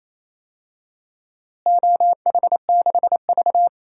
Morse code, heard as a keyed tone, about 28 wpm, 700 Hz. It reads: OH6V
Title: DXCC-100 Call Signs - 4 Characters - Repeated 2x - 28wpm